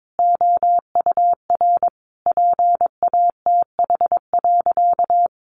Morse code, heard as a keyed tone, about 22 wpm, 700 Hz. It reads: OUR PATH.
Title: Sentences from Top 1400 Words 22wpm